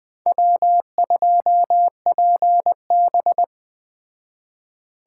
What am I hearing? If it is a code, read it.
Morse code, heard as a keyed tone, about 20 wpm, 700 Hz. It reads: W2PB
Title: Speed Racing - USA 1x2 Call Signs 17wpm